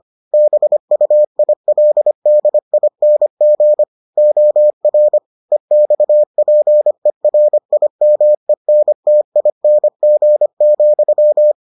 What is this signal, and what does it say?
Morse code, heard as a keyed tone, about 25 wpm, 600 Hz. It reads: BUILDING OR EXPERIMENTING,